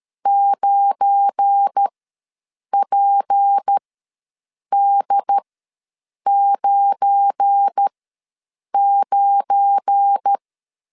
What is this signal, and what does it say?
Morse code, heard as a keyed tone, about 13 wpm, spaced as for 7 wpm, 800 Hz. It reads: 9PD99
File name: LEZIONI / 7 GF9Z / 7 GF9 ws 12wpm 3344